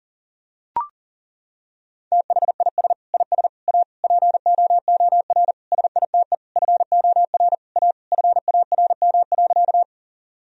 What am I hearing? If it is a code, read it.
Morse code, heard as a keyed tone, about 40 wpm, 700 Hz. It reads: THIS IS A POOR SITE FOR A FARM.
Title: Sentences from Top 1000 Words - Review Part 4 - 40wpm